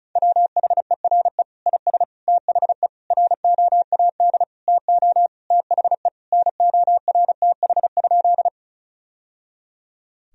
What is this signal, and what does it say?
Morse code, heard as a keyed tone, about 36 wpm, 700 Hz. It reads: WHERE IS THE ROAD TO THE NORTH?
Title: Sentences from Top 1000 Words - Review Part 2 - 35wpm